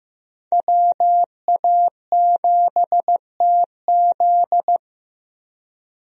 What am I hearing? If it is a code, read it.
Morse code, heard as a keyed tone, about 15 wpm, 700 Hz. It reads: WA7TZ
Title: USA Call Signs - 5 Characters - Repeated 2x - 15wpm